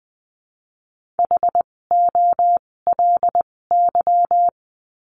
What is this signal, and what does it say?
Morse code, heard as a keyed tone, about 20 wpm, spaced as for 16 wpm, 700 Hz. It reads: HOLY
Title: 256 Lesson 42a - Words 12-20wpm